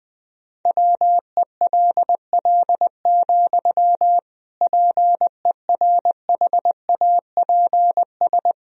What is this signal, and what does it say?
Morse code, heard as a keyed tone, about 20 wpm, 700 Hz. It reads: WELL, PERHAPS
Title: Sentences from Top 1000 Words - Review Part 2 - 20wpm